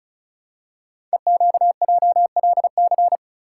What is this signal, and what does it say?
Morse code, heard as a keyed tone, about 36 wpm, 700 Hz. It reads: EQJLC
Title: Code Groups - 5 Letter:Numbers - 35wpm